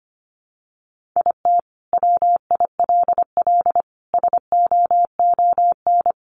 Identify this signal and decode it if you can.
Morse code, heard as a keyed tone, about 25 wpm, 700 Hz. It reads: IT WILL SOON
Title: Sentences from Top 1000 Words 25wpm